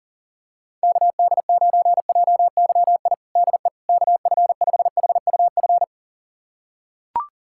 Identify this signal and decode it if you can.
Morse code, heard as a keyed tone, about 40 wpm, 700 Hz. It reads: KD9JYI DE KF5HUF
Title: USA Callsign DE Callsign 40wpm